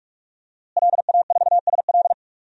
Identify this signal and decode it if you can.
Morse code, heard as a keyed tone, about 46 wpm, 700 Hz. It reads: LA4SL
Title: DXCC-100 Call Signs - 5 Characters - Repeated 3x - 45wpm